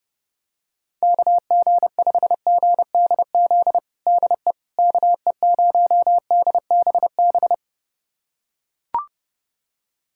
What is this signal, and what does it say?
Morse code, heard as a keyed tone, about 30 wpm, 700 Hz. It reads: KG5GDZ DE KE0DBB
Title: USA Callsign DE Callsign - 6 Characters - 30wpm